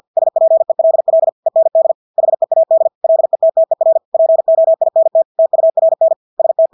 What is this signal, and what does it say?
Morse code, heard as a keyed tone, about 50 wpm, 650 Hz. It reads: SPELL END HEAD LETTER POINT TURN ST
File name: Engelske ord 50 wpm